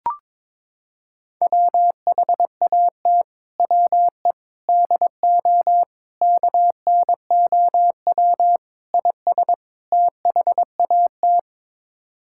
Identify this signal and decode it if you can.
Morse code, heard as a keyed tone, about 22 wpm, 700 Hz. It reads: WHAT WE DO KNOW IS THAT
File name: practice / 22wpm / Sets of 6 Words - 22wpm